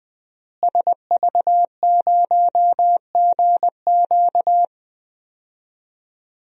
Morse code, heard as a keyed tone, about 21 wpm, 700 Hz. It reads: SV0GQ
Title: DXCC-100 Call Signs - 5 Characters - Repeated 3x - 20wpm